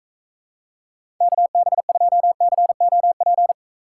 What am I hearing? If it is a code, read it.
Morse code, heard as a keyed tone, about 34 wpm, 700 Hz. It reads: KB2COP
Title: Speed Racing - USA 2x3 Call Signs 28wpm